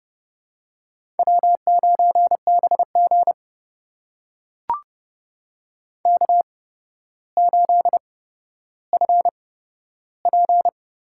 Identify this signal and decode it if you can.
Morse code, heard as a keyed tone, about 30 wpm, spaced as for 13 wpm, 700 Hz. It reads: W9BG K8FP
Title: USA Call Signs - 4 Characters - 8x Character Spacing - 30wpm